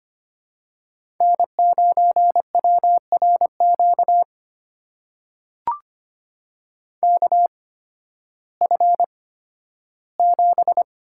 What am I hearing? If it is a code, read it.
Morse code, heard as a keyed tone, about 25 wpm, spaced as for 13 wpm, 700 Hz. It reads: N9WRQ KF7
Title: USA Call Signs - 8x Character Spacing - 25wpm